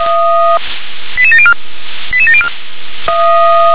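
Five-tone selective call (ZVEI), heard as quick runs of five tones, tones of about 70 ms. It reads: 88704 88704
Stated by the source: Five-tone ZVEI selective call received on SDR.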